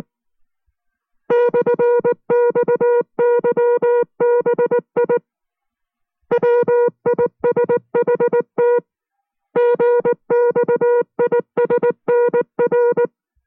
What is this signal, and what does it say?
Morse code, heard as a keyed tone, about 19 wpm, 480 Hz. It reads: /XYBI WISHT GXISNR